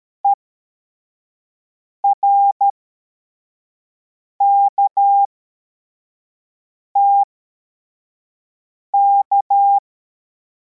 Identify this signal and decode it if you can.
Morse code, heard as a keyed tone, about 13 wpm, spaced as for 4 wpm, 800 Hz. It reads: ERKTK